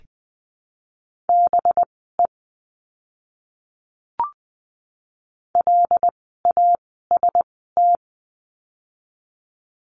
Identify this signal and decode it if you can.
Morse code, heard as a keyed tone, about 20 wpm, spaced as for 14 wpm, 700 Hz. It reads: BE LAST
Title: 237 Lesson 38f - Letter q - Words 20wpm